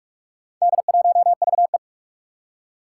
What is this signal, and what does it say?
Morse code, heard as a keyed tone, about 44 wpm, 700 Hz. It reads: D1VE
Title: Four Challenging Letter-Number Combinations 45wpm